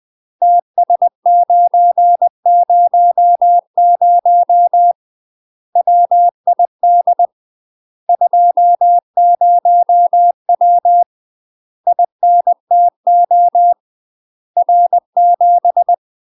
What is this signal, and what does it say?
Morse code, heard as a keyed tone, about 20 wpm, 700 Hz. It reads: TS900 WID 20W INTO R7
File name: UK MASTER QSOs TRAINING / 20WPM / UK MASTER QSOs 0029